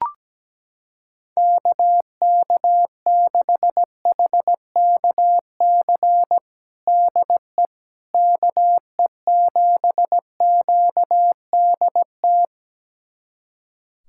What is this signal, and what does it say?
Morse code, heard as a keyed tone, about 17 wpm, 700 Hz. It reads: KK6HKC DE KE7QDT
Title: USA Callsign DE Callsign - 6 Characters - 17wpm